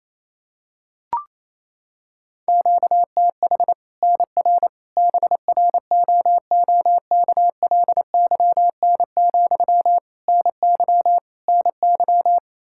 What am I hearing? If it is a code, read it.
Morse code, heard as a keyed tone, about 28 wpm, 700 Hz. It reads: QTH NR BROOKLYN, NY NY
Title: Practice QSO - Exchange 1 - Beginning - 28wpm